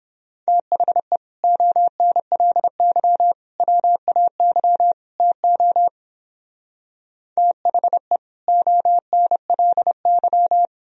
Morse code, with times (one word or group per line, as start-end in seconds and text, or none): 0.5-1.2 THE
1.4-3.3 ONLY
3.6-4.9 WAY
5.2-5.9 TO
7.4-8.2 THE
8.5-10.7 ONLY